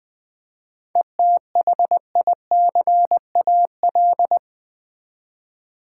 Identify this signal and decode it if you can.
Morse code, heard as a keyed tone, about 20 wpm, 700 Hz. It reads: ETHICAL
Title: 7 Letter Words - First 3 Letters Given - 20wpm